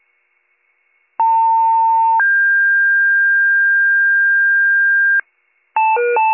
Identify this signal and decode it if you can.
Two-tone page, heard as a one-second tone, then a three-second tone, about 900 then 1600 Hz, with noise. It is heard from 1.2 to 5.2 s.